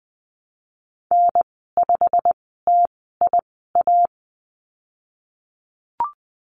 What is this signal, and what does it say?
Morse code, heard as a keyed tone, about 20 wpm, spaced as for 15 wpm, 700 Hz. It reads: N5TIA
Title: 161 Lesson 26g - Letter g - Call Signs 20wpm